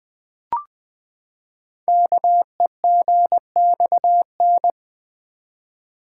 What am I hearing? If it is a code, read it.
Morse code, heard as a keyed tone, about 20 wpm, 700 Hz. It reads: KEGXN
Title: Code Groups - 5 Letters - 20wpm